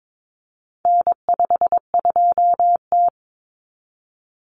Morse code, heard as a keyed tone, about 22 wpm, 700 Hz. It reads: N52T
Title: Common Cut Numbers - 4-Digit 22wpm